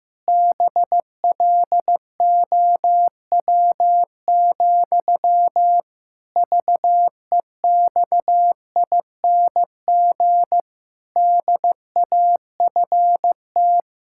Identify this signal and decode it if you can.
Morse code, heard as a keyed tone, about 15 wpm, 700 Hz. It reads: BLOW, VEXING DAFT
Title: Pangram Challenge v2 - 15wpm